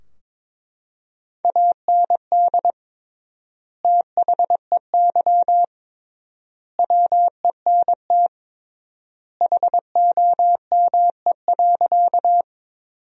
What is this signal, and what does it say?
Morse code, heard as a keyed tone, about 22 wpm, 700 Hz. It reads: AND THEY WENT HOME.
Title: 3x Word-Spacing - Sentences from Top 200 Words - 22wpm